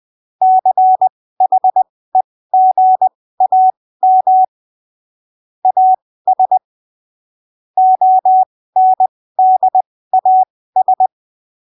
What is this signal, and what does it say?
Morse code, heard as a keyed tone, about 20 wpm, spaced as for 15 wpm, 750 Hz. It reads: CHEGAM AS ONDAS
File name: arquivos / texto / texto 13 20ppm